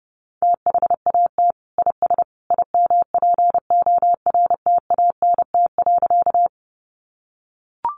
Morse code, heard as a keyed tone, about 28 wpm, 700 Hz. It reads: THAT IS IMPORTANT.
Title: Sentences from Top 600 Words 30wpm